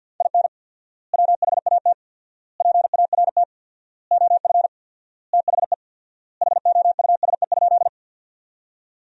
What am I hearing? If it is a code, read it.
Morse code, heard as a keyed tone, about 49 wpm, 700 Hz. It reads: IN WHAT PART OF THE HOUSE?
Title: 4x Word-Spacing - Sentences from Top 200 Words - 50wpm